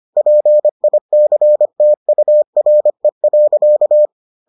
Morse code, heard as a keyed tone, about 25 wpm, 600 Hz. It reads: PICTURE.